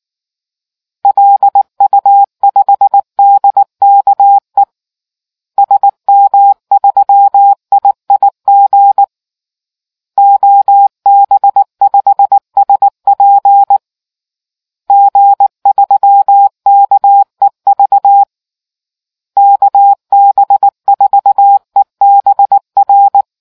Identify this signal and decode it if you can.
Morse code, heard as a keyed tone, about 19 wpm, 800 Hz. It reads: LU5DKE SM3IIG OB5SP G3KEV KB4EBR